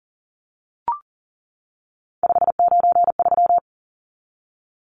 Morse code, heard as a keyed tone, about 40 wpm, 700 Hz. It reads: 593
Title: Numbers - 3-Digit 40wpm